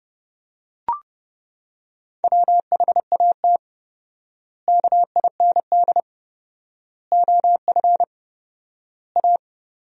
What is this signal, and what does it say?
Morse code, heard as a keyed tone, about 32 wpm, 700 Hz. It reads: WHAT KIND OF A
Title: 4x Word-Spacing - Sentences from Top 400 Words - 30wpm